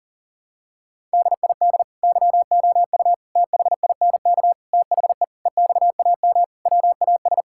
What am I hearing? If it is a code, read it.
Morse code, heard as a keyed tone, about 40 wpm, 700 Hz. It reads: DID YOU THINK THE EXAM WAS
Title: Sentences from Top 1000 Words - Review Part 5 - 40wpm